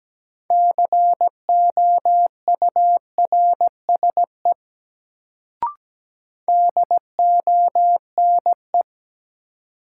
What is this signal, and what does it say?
Morse code, heard as a keyed tone, about 17 wpm, 700 Hz. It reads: COURSE DONE
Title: Sentences from Top 500 Words - New Word Drill - 17wpm